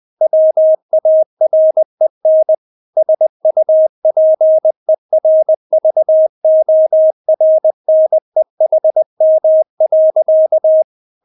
Morse code, heard as a keyed tone, about 20 wpm, 630 Hz. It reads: WAREN SUPERVORNEHM.
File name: Witz1 20wpm